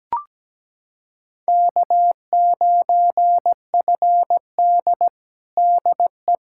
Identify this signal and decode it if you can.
Morse code, heard as a keyed tone, about 17 wpm, 700 Hz. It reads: K9FD DE
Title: Practice QSO - Exchange 4 - Ending - 17wpm